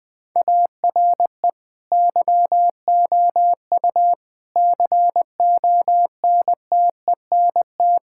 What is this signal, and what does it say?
Morse code, heard as a keyed tone, about 21 wpm, 700 Hz. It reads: ARE YOU CONTENT
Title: Sentences from Top 1000 Words - Review Part 6 - 20wpm